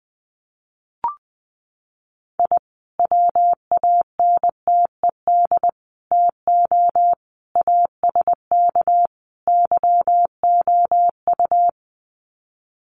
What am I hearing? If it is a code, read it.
Morse code, heard as a keyed tone, about 19 wpm, 700 Hz. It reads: I WANTED TO ASK YOU
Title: Sets of 5 Words Encore2 20wpm